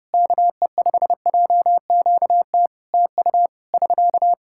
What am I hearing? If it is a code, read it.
Morse code, heard as a keyed tone, about 30 wpm, 700 Hz. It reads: KE5JQT TU <SK>
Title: Practice QSO - Exchange 4 - Ending - 30wpm